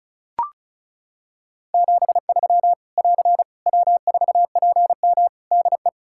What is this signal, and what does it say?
Morse code, heard as a keyed tone, about 36 wpm, 700 Hz. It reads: 73 <AR> W4PM DE
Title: Practice QSO - Exchange 4 - Ending - 35wpm